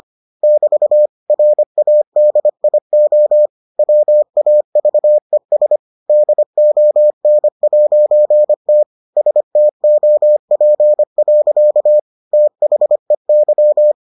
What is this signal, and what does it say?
Morse code, heard as a keyed tone, about 25 wpm, 600 Hz. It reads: <BT> RADIO WAVES DON'T STOP. THEY